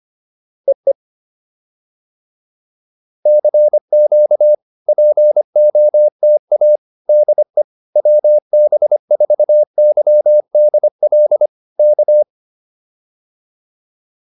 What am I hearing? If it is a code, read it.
Morse code, heard as a keyed tone, about 25 wpm, 600 Hz. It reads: EE CQ POTA DE WB4YDL K